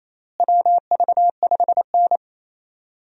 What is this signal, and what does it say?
Morse code, heard as a keyed tone, about 28 wpm, 700 Hz. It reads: WV5N
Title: USA Call Signs - 4 Characters - Repeated 3x - 28wpm